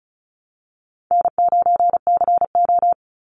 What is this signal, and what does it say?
Morse code, heard as a keyed tone, about 36 wpm, 700 Hz. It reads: N9CO
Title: Easy Call Signs Repeated 3x - 35wpm